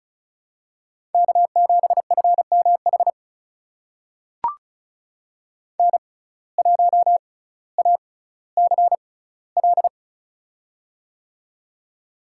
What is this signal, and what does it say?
Morse code, heard as a keyed tone, about 34 wpm, spaced as for 18 wpm, 700 Hz. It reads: K7FMH N1ACL
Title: USA Call Signs - 5 Characters - 6x Character Spacing - 35wpm